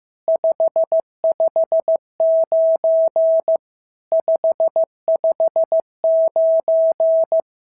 Morse code, heard as a keyed tone, about 15 wpm, 650 Hz. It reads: 559 559
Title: SOTA QSO Practice 15wpm